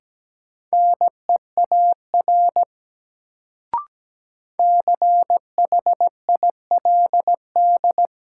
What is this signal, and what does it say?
Morse code, heard as a keyed tone, about 17 wpm, 700 Hz. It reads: NEAR CHILD
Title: Sets of 5 Words Encore - New Word Drill - 17wpm